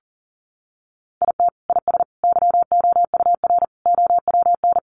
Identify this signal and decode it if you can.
Morse code, heard as a keyed tone, about 37 wpm, 700 Hz. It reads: IT IS YOUR OWN
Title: Sentences from Top 600 Words 40wpm